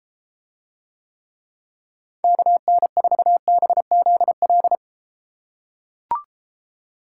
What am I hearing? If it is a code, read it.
Morse code, heard as a keyed tone, about 31 wpm, 700 Hz. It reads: KN4BZL